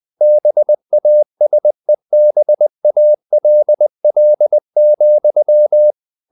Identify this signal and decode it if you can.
Morse code, heard as a keyed tone, about 20 wpm, 600 Hz. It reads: BASEBALL,
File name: the old man and the sea / the old man and the sea part 1 20wpm